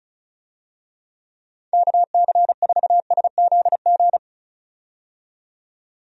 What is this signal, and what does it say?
Morse code, heard as a keyed tone, about 36 wpm, 700 Hz. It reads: KC4SZG
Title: USA Call Signs - 6 Characters - 35wpm